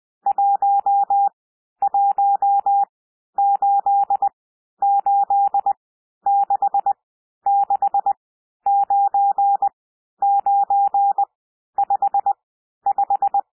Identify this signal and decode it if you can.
Morse code, heard as a keyed tone, about 20 wpm, spaced as for 11 wpm, 800 Hz. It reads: 1188669955
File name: cwmp3 / N1 SOUND NUMBERS 20 wpm